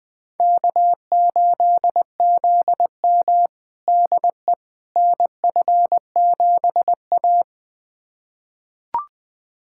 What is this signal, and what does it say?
Morse code, heard as a keyed tone, about 20 wpm, 700 Hz. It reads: K8ZM DE NF7A
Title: USA Callsign DE Callsign - 4 Characters - 20wpm